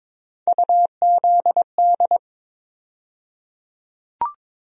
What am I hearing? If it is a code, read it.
Morse code, heard as a keyed tone, about 21 wpm, 700 Hz. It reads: UZD
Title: USA Call Sign Postfixes - 3 Letters 22wpm